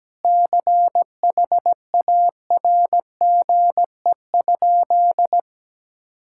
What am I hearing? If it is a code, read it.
Morse code, heard as a keyed tone, about 17 wpm, 700 Hz. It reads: CHARGE?
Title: Sentences from Top 1000 Words - Review Part 6 - 17wpm